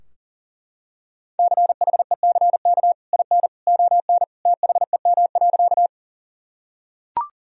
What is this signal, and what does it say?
Morse code, heard as a keyed tone, about 40 wpm, 700 Hz. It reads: CHECK IN ON THEM.
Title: Sentences from Top 1000 Words - Review Part 3 - 40wpm